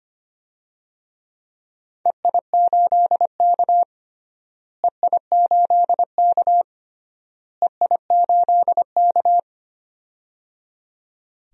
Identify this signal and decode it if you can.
Morse code, heard as a keyed tone, about 25 wpm, 700 Hz. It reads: EI8K EI8K EI8K
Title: DXCC-100 Call Signs - 4 Characters - Repeated 3x - 25wpm